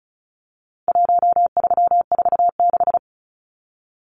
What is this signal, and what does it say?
Morse code, heard as a keyed tone, about 34 wpm, 700 Hz. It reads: 1346
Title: Numbers - Contest Style - 4-Digit 35wpm